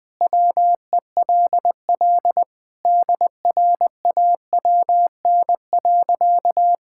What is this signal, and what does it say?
Morse code, heard as a keyed tone, about 20 wpm, 700 Hz. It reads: WELL DRAWN.
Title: All-Age Appropriate Jokes - 20wpm